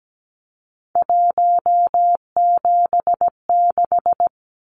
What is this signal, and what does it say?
Morse code, heard as a keyed tone, about 17 wpm, 700 Hz. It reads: 176
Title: Numbers - 3-Digit 17wpm